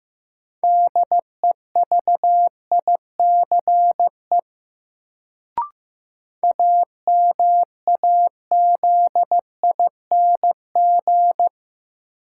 Morse code, heard as a keyed tone, about 15 wpm, 700 Hz. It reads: DEVICE AMAZING
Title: Sentences from Top 1500 Words - New Word Drill - 15wpm